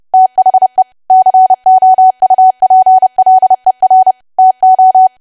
Morse code, heard as a keyed tone, about 30 wpm, 750 Hz. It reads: THE COUPLER TO
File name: cwa / W1AW-Level3 / 30WPM